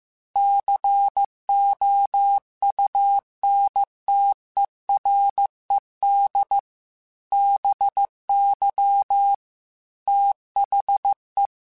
Morse code, heard as a keyed tone, about 15 wpm, 800 Hz. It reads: COUNTERED BY THE